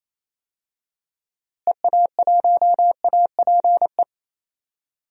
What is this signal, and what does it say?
Morse code, heard as a keyed tone, about 28 wpm, 700 Hz. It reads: EA1APE
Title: DXCC-100 Call Signs - 6 Characters - Repeated 2x - 28wpm